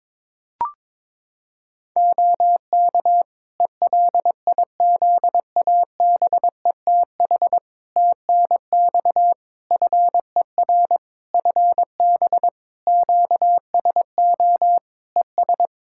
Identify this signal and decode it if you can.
Morse code, heard as a keyed tone, about 22 wpm, 700 Hz. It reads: OK ELIZABETH TNX FER FB QSO ES
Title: Practice QSO - Exchange 4 - Ending - 22wpm